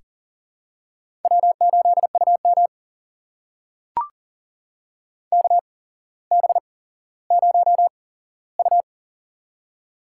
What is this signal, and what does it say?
Morse code, heard as a keyed tone, about 40 wpm, spaced as for 17 wpm, 700 Hz. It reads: W8UM KB0U
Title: USA Call Signs - 4 Characters - 8x Character Spacing - 40wpm